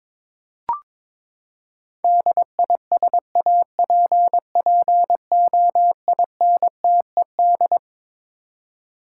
Morse code, heard as a keyed tone, about 22 wpm, 700 Hz. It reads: DISAPPOINTED
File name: Twelve Letter Words - 22wpm